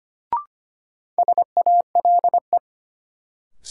Morse code, heard as a keyed tone, about 25 wpm, 700 Hz. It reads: SALE
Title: Sentences from Top 1400 Words - New Word Drill - 25wpm